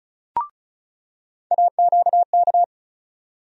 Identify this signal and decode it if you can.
Morse code, heard as a keyed tone, about 34 wpm, 700 Hz. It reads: AQK
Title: USA Call Sign Postfixes - 3 Letters 35wpm